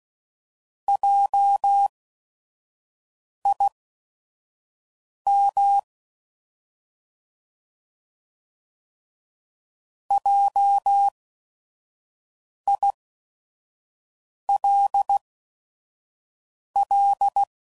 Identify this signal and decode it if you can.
Morse code, heard as a keyed tone, about 16 wpm, spaced as for 5 wpm, 800 Hz. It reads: JIM JILL